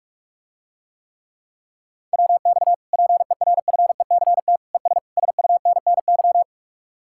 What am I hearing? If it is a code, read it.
Morse code, heard as a keyed tone, about 49 wpm, 700 Hz. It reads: WX PERFECT ES SUNNY